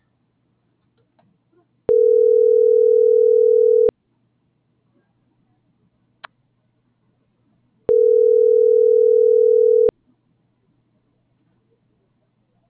Ringback tone (North America), 440 + 480 Hz.